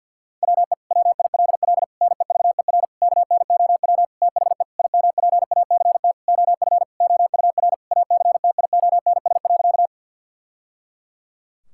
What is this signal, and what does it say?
Morse code, heard as a keyed tone, about 52 wpm, 700 Hz. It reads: WE WILL NEVER KNOW THE IMPACT OF OUR ACTIONS.
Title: Sentences from Top 1000 Words - Review Part 6 - 50wpm